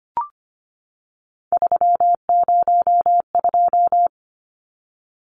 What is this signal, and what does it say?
Morse code, heard as a keyed tone, about 25 wpm, 700 Hz. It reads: 302
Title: Numbers - 3-Digit 25wpm